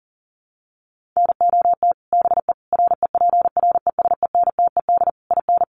Morse code, heard as a keyed tone, about 40 wpm, 700 Hz. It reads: NOT BE REPRESENTED IN
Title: Sets of 5 Words 40wpm